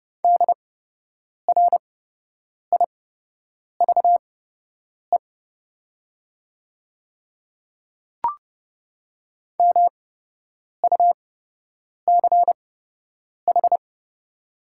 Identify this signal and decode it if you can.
Morse code, heard as a keyed tone, about 30 wpm, spaced as for 8 wpm, 700 Hz. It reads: DRIVE MUCH
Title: 8x Character Spacing - Top 500 Words - 30wpm